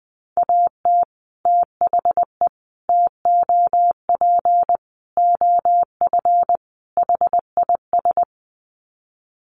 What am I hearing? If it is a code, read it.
Morse code, heard as a keyed tone, about 20 wpm, 700 Hz. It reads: AT THE TOP OF HIS
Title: Sets of 5 Words Encore 20wpm